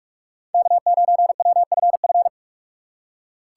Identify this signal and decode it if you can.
Morse code, heard as a keyed tone, about 45 wpm, 700 Hz. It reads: K9WFF